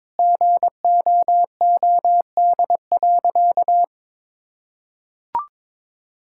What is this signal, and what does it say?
Morse code, heard as a keyed tone, about 22 wpm, 700 Hz. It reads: GOOD.
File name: practice / 22wpm / Sentences with 4-Word Phrases - 22wpm